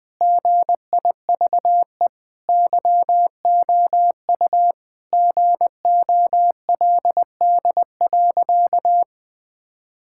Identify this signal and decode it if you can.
Morse code, heard as a keyed tone, about 20 wpm, 700 Hz. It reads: GIVE YOU GOLD.
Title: Sentences from Top 1000 Words - Review Part 6 - 20wpm